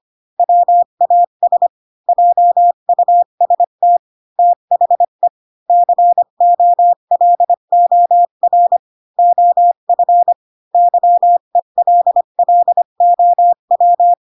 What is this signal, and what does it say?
Morse code, heard as a keyed tone, about 25 wpm, 700 Hz. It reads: WAS JUST THE COLOR OF YELLOW